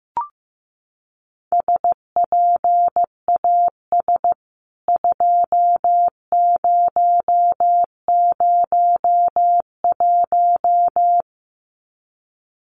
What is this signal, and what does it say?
Morse code, heard as a keyed tone, about 15 wpm, 700 Hz. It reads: SPAS 2001
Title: CWT Exchange 15wpm